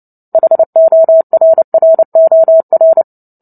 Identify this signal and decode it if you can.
Morse code, heard as a keyed tone, about 30 wpm, 650 Hz. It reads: HORROR
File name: Book Two / 03-Chapter 3 - 29 WPM